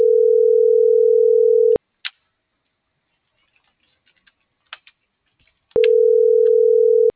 Ringback tone (North America), 440 + 480 Hz.